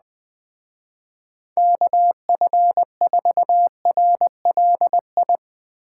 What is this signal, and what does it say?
Morse code, heard as a keyed tone, about 19 wpm, 700 Hz. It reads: KF4RLI
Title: Speed Racing - USA Call Signs 17wpm